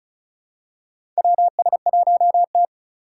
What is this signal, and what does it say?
Morse code, heard as a keyed tone, about 36 wpm, 700 Hz. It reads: WS1T
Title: USA Call Signs - 2x Character Spacing - 35wpm